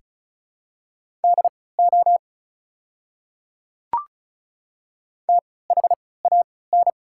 Morse code, heard as a keyed tone, about 36 wpm, spaced as for 20 wpm, 690 Hz. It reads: DO THAN
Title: 3x Character Spacing - Top 100 Words - 35wpm